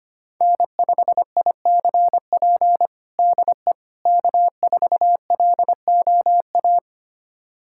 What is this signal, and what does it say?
Morse code, heard as a keyed tone, about 25 wpm, 700 Hz. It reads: N5ICP DE K4LOA
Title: USA Callsign DE Callsign - 5 Characters - 25wpm